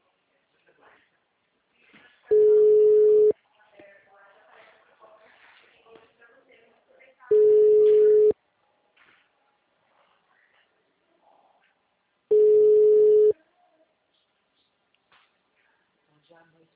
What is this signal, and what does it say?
Special information tone, with speech.